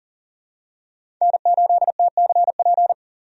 Morse code, heard as a keyed tone, about 40 wpm, 700 Hz. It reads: N8TCP
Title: USA Call Signs - 5 Characters - 40wpm